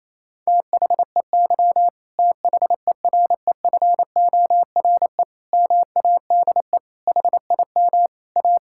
Morse code, heard as a keyed tone, about 28 wpm, 700 Hz. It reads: THEY THEREFORE MADE HIM A
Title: Sentences from Top 1000 Words - Review Part 5 - 28wpm